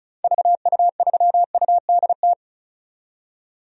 Morse code, heard as a keyed tone, about 34 wpm, 700 Hz. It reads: VU3UDT
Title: DXCC-100 Call Signs - 6 Characters - 35wpm